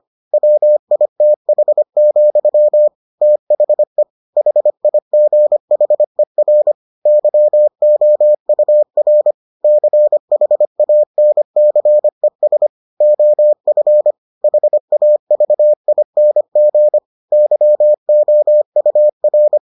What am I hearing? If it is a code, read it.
Morse code, heard as a keyed tone, about 25 wpm, 600 Hz. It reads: WITH, THE HIGHER YOUR CHANCES OF HAVING YOUR